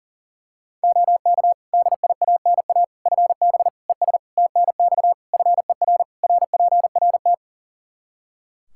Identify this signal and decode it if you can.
Morse code, heard as a keyed tone, about 40 wpm, 700 Hz. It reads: OK DIANA FB ES TNX FER RPRT
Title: Practice QSO - Exchange 2 - Info - 40wpm